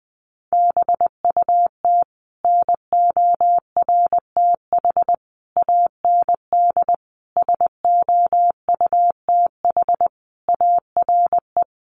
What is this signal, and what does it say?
Morse code, heard as a keyed tone, about 20 wpm, 700 Hz. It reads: BUT NORTH AND SOUTH ARE
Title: Sentences from Top 1100 Words 20wpm